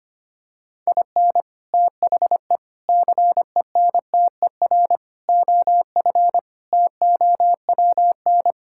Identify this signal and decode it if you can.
Morse code, heard as a keyed tone, about 25 wpm, 700 Hz. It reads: IN THE CENTER OF TOWN
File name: practice / 25wpm / Sentences with 4-Word Phrases - 25wpm